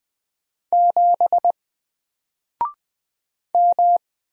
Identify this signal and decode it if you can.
Morse code, heard as a keyed tone, about 20 wpm, 700 Hz. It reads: 7 M
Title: Single Character (Letter:Number:Punctuation:Prosign) 20wpm